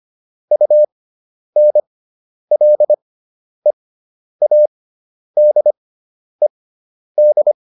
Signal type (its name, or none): Morse code